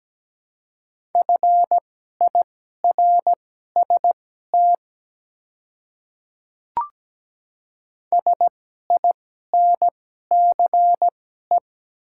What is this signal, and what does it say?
Morse code, heard as a keyed tone, about 17 wpm, spaced as for 12 wpm, 700 Hz. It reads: FIRST SINCE
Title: 2x Character Spacing - Top 300 Words - 17wpm